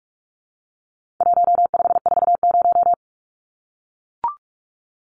Morse code, heard as a keyed tone, about 44 wpm, 700 Hz. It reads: 1540
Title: Numbers - 4-Digit 45wpm